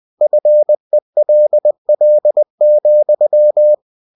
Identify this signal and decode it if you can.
Morse code, heard as a keyed tone, about 20 wpm, 600 Hz. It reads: FELL,
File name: winnie-20wpm